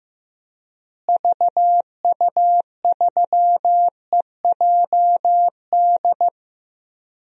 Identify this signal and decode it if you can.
Morse code, heard as a keyed tone, about 15 wpm, 700 Hz. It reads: VU3EJD
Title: DXCC-100 Call Signs - 6 Characters - Repeated 3x - 15wpm